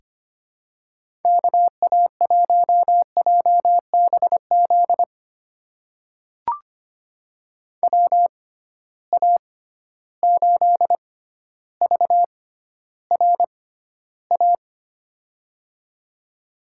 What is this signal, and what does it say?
Morse code, heard as a keyed tone, about 25 wpm, spaced as for 13 wpm, 700 Hz. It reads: KA1JBZ WA8VRA
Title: USA Call Signs - 6 Characters - 6x Character Spacing - 25wpm